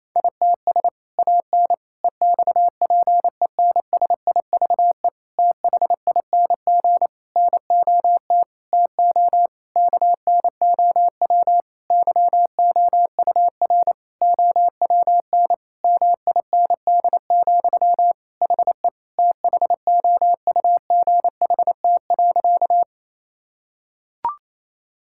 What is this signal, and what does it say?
Morse code, heard as a keyed tone, about 28 wpm, 700 Hz. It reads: ITS AN EXPENSIVE THING NOT TO KNOW YOUR OWN MIND, HE THOUGHT.
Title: Sentences from Top 1000 Words - Review Part 6 - 28wpm